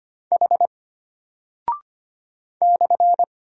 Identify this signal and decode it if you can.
Morse code, heard as a keyed tone, about 25 wpm, 700 Hz. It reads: H /
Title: Single Character (Letter:Number:Punctuation:Prosign) Alt 25wpm